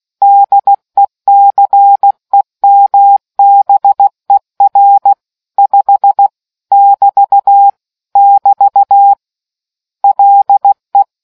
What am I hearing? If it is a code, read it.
Morse code, heard as a keyed tone, about 16 wpm, 800 Hz. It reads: DECEMBER 5 <BT> <BT> LE